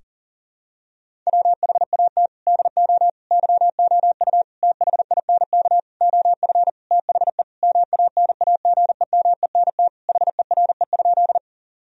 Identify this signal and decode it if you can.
Morse code, heard as a keyed tone, about 40 wpm, 700 Hz. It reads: WHAT DO YOU THINK OF THE MANAGEMENT HERE?
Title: Sentences from Top 1000 Words - Review Part 4 - 40wpm